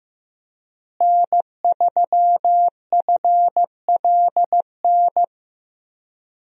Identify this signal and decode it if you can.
Morse code, heard as a keyed tone, about 15 wpm, 700 Hz. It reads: N3FLN
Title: USA Call Signs - 5 Characters - Repeated 2x - 15wpm